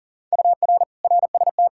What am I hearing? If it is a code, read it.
Morse code, heard as a keyed tone, about 40 wpm, 700 Hz. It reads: UR RST
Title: Practice QSO - Exchange 1 - Beginning - 40wpm